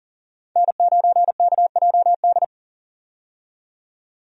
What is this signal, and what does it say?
Morse code, heard as a keyed tone, about 40 wpm, 700 Hz. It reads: N9KJD
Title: USA Call Signs - 5 Characters - 40wpm